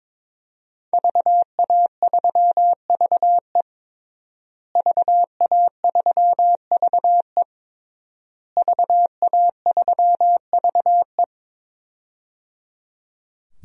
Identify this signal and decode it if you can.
Morse code, heard as a keyed tone, about 22 wpm, 700 Hz. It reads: VA3VE VA3VE VA3VE
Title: DXCC-100 Call Signs - 5 Characters - Repeated 3x - 22wpm